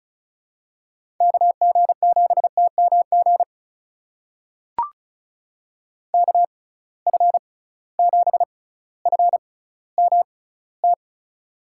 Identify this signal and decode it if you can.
Morse code, heard as a keyed tone, about 36 wpm, spaced as for 18 wpm, 700 Hz. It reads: KG7TMG KF7FMT
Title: USA Call Signs - 6x Character Spacing - 35wpm